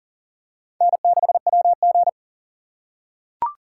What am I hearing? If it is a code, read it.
Morse code, heard as a keyed tone, about 40 wpm, 700 Hz. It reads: N6WG
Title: USA Call Signs - 4 Characters - 2x Character Spacing - 40wpm